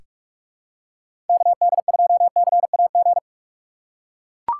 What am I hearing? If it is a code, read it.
Morse code, heard as a keyed tone, about 43 wpm, 700 Hz. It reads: KD2CAG
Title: USA Call Signs - 6 Characters - 6x Character Spacing - 45wpm